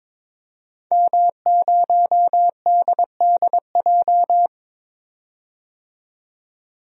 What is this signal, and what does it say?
Morse code, heard as a keyed tone, about 22 wpm, 700 Hz. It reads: M0DDJ
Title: DXCC-100 Call Signs - 5 Characters - 22wpm